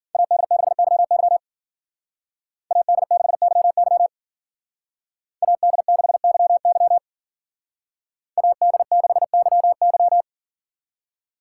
Morse code, heard as a keyed tone, about 48 wpm, 700 Hz. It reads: AD6YY AD6YY AD6YY AD6YY
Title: Speed Racing - USA 2x2 Call Signs 40wpm